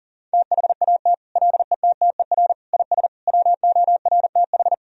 Morse code, heard as a keyed tone, about 40 wpm, 700 Hz. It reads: THAT LETTER IS WORTH
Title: Sentences from Top 1000 Words - Review Part 3 - 40wpm